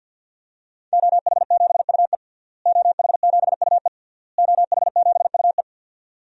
Morse code, heard as a keyed tone, about 52 wpm, 700 Hz. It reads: OH7UE OH7UE OH7UE
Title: DXCC-100 Call Signs - 5 Characters - Repeated 3x - 50wpm